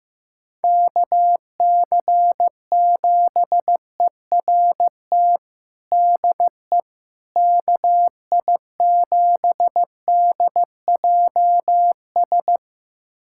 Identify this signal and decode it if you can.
Morse code, heard as a keyed tone, about 16 wpm, 700 Hz. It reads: KC7ERT DE KI7DJS
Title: USA Callsign DE Callsign - 6 Characters - 15wpm